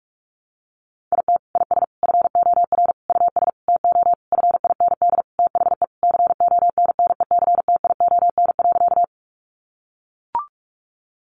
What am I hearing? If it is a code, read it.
Morse code, heard as a keyed tone, about 44 wpm, 700 Hz. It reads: IT IS FOR US TO FIND THE CONNECTION.
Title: Sentences from Top 1300 Words 45wpm